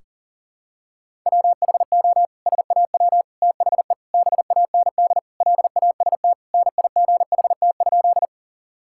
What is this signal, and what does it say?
Morse code, heard as a keyed tone, about 37 wpm, 700 Hz. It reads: WHO SAW THE BAND LAST NIGHT?
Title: Sentences from Top 1600 Words 40wpm